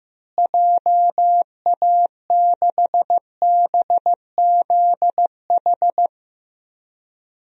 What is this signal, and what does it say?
Morse code, heard as a keyed tone, about 15 wpm, 700 Hz. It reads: JA6BZH
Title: DXCC-100 Call Signs - 6 Characters - Repeated 2x - 15wpm